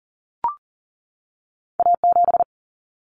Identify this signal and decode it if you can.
Morse code, heard as a keyed tone, about 40 wpm, 700 Hz. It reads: A7
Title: Common Cut Numbers - 2-Digit 40wpm